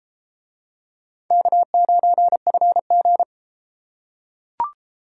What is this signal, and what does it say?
Morse code, heard as a keyed tone, about 32 wpm, 700 Hz. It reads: K9FG